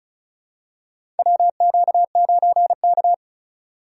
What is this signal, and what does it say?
Morse code, heard as a keyed tone, about 36 wpm, 700 Hz. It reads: WQ9K
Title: USA Call Signs - 4 Characters - 6x Character Spacing - 35wpm